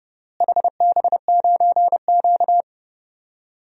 Morse code, heard as a keyed tone, about 29 wpm, 700 Hz. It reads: HB9Q
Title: DXCC-100 Call Signs - 4 Characters - Repeated 2x - 30wpm